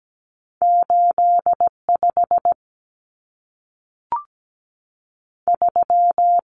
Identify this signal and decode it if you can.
Morse code, heard as a keyed tone, about 17 wpm, 700 Hz. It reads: 8 5 3
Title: Numbers - 2-Digit 17wpm